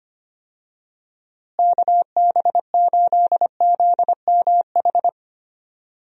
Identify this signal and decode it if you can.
Morse code, heard as a keyed tone, about 25 wpm, 700 Hz. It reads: KB8ZMH
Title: USA Call Signs - 6 Characters - Repeated 3x - 25wpm